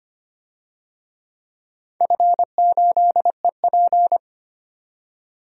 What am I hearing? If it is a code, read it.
Morse code, heard as a keyed tone, about 25 wpm, 700 Hz. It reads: F8EP
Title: DXCC-100 Call Signs - 4 Characters - 25wpm